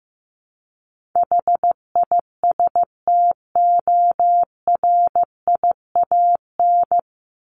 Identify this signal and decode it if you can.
Morse code, heard as a keyed tone, about 15 wpm, 700 Hz. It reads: HISTORIAN
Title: Top 1751-2000 Words 15wpm